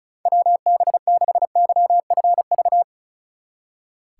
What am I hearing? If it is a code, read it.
Morse code, heard as a keyed tone, about 32 wpm, 700 Hz. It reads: WB6YFV
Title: Speed Racing - USA 2x3 Call Signs 35wpm